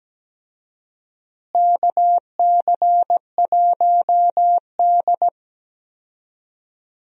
Morse code, heard as a keyed tone, about 17 wpm, 700 Hz. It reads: KC1D
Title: USA Call Signs - 4 Characters - Repeated 2x - 17wpm